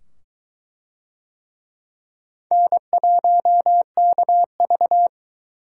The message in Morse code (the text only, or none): N1KV